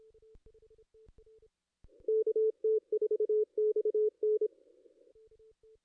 Morse code, heard as a keyed tone, about 28 wpm, 440 Hz. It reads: KT4XN
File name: ss-cw-k5tr-op-n5rz